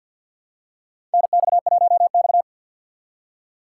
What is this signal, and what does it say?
Morse code, heard as a keyed tone, about 50 wpm, 700 Hz. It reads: NX1X
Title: USA Call Signs - 2x Character Spacing - 50wpm